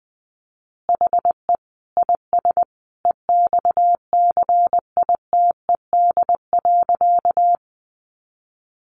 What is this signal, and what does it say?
Morse code, heard as a keyed tone, about 20 wpm, 700 Hz. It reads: HE IS EXCITED.
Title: Additional Easy Phrases 20wpm